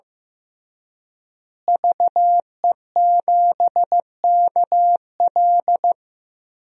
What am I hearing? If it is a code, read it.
Morse code, heard as a keyed tone, about 15 wpm, 700 Hz. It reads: VE7KL
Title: DXCC-100 Call Signs - 5 Characters - Repeated 3x - 15wpm